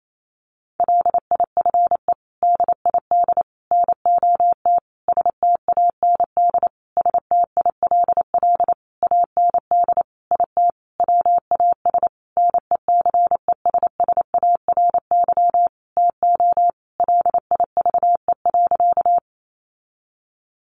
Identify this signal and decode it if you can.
Morse code, heard as a keyed tone, about 28 wpm, 700 Hz. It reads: LIFE DID NOT STAND STILL AND IT WAS NECESSARY TO LIVE.
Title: Sentences from Top 800 Words 28wpm